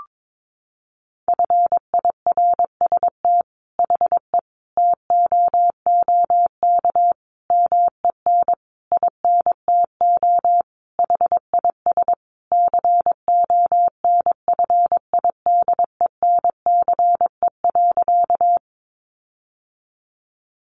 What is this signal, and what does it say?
Morse code, heard as a keyed tone, about 22 wpm, 700 Hz. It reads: FIRST HE TOOK MEN INTO HIS CONFIDENCE.
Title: Sentences from Top 1500 Words 22wpm